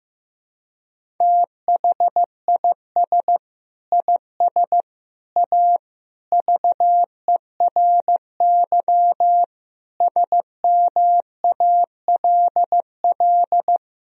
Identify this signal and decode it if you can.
Morse code, heard as a keyed tone, about 15 wpm, 700 Hz. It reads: THIS IS A VERY SMALL
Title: Sentences from Top 1000 Words - Review Part 1 - 15wpm